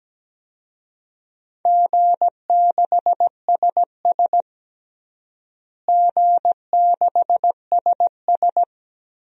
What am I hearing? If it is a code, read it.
Morse code, heard as a keyed tone, about 17 wpm, 700 Hz. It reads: G6SS G6SS
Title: DXCC-100 Call Signs - 4 Characters - Repeated 3x - 17wpm